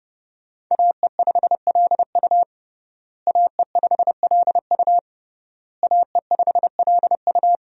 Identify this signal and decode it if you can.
Morse code, heard as a keyed tone, about 30 wpm, 700 Hz. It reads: AE5LU AE5LU AE5LU
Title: USA Call Signs - 5 Characters - Repeated 3x - 30wpm